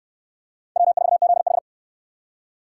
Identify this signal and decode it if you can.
Morse code, heard as a keyed tone, about 68 wpm, 700 Hz. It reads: W4ZH